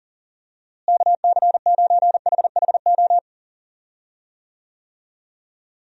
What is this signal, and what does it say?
Morse code, heard as a keyed tone, about 40 wpm, 700 Hz. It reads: KC9HHO
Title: USA Call Signs - 6 Characters - Repeated 2x - 40wpm